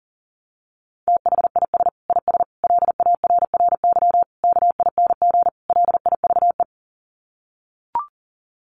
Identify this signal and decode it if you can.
Morse code, heard as a keyed tone, about 40 wpm, 700 Hz. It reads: THIS IS LARRY KING LIVE
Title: Sets of 5 Words Encore2 40wpm